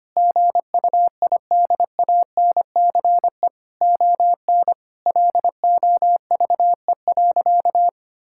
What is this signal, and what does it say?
Morse code, heard as a keyed tone, about 25 wpm, 700 Hz. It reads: GUIDANCE ON LOVE.
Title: Sentences from Top 1600 Words 25wpm